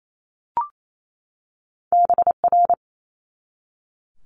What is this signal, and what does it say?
Morse code, heard as a keyed tone, about 28 wpm, 700 Hz. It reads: BR
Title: Multi-letter Phonemes 28wpm